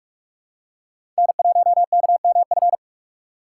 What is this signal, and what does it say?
Morse code, heard as a keyed tone, about 46 wpm, 700 Hz. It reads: N1KMF
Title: USA Call Signs - 5 Characters - 4x Character Spacing - 45wpm